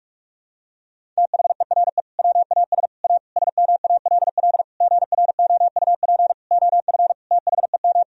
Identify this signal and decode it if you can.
Morse code, heard as a keyed tone, about 46 wpm, 700 Hz. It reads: THERE WAS A SMALL GROUP OF THEM
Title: Sentences from Top 1000 Words - Review Part 1 - 45wpm